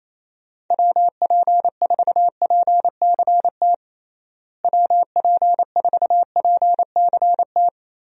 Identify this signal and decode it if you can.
Morse code, heard as a keyed tone, about 27 wpm, 700 Hz. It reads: WP4PCT WP4PCT
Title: USA Call Signs - 6 Characters - Repeated 3x - 28wpm